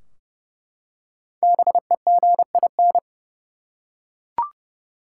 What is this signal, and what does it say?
Morse code, heard as a keyed tone, about 31 wpm, 700 Hz. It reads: BEGIN
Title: Five Letter Words - 30wpm